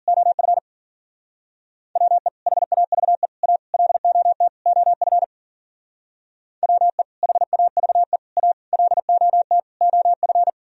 Morse code, heard as a keyed tone, about 50 wpm, 700 Hz. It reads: OF WE HAVE A LOT OF WE HAVE A LOT OF